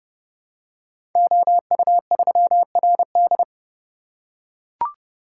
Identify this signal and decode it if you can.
Morse code, heard as a keyed tone, about 30 wpm, 700 Hz. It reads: OU3RD
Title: Code Groups - 5 Letter:Numbers - 30wpm